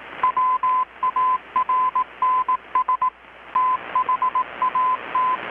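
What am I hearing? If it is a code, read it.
Morse code, heard as a keyed tone, about 18 wpm, 1020 Hz. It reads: WARNS THAT
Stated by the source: Morse from KPH's Night of Nights XX maritime radiotelegraphy event, received in Chiapas, Mexico about 3,800 km away on a Kenwood R-600 with a grounded 10 m long-wire antenna.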